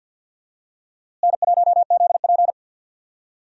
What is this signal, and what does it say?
Morse code, heard as a keyed tone, about 52 wpm, 700 Hz. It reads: N1ZP
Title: USA Call Signs - 4 Characters - Repeated 3x - 50wpm